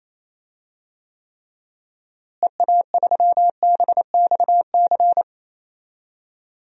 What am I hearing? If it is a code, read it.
Morse code, heard as a keyed tone, about 28 wpm, 700 Hz. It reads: EA3BXC
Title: DXCC-100 Call Signs - 6 Characters - Repeated 3x - 28wpm